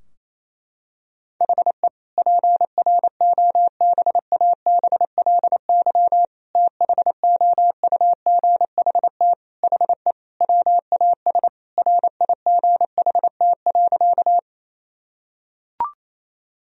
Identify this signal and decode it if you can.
Morse code, heard as a keyed tone, about 28 wpm, 700 Hz. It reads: HE PROBABLY THOUGHT HE WAS RIGHT.
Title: Sentences from Top 1000 Words - Review Part 4 - 28wpm